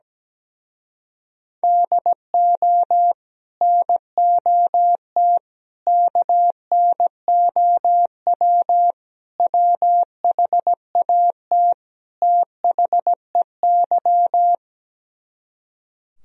Morse code, heard as a keyed tone, about 17 wpm, 700 Hz. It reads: DO NOT KNOW WHAT THEY
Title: Speed-Racing - Sets of 5 Words 17wpm